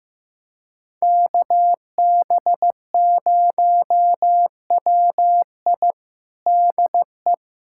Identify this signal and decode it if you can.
Morse code, heard as a keyed tone, about 15 wpm, 700 Hz. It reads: KB0WI DE
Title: USA Callsign DE Callsign - 5 Characters - 15wpm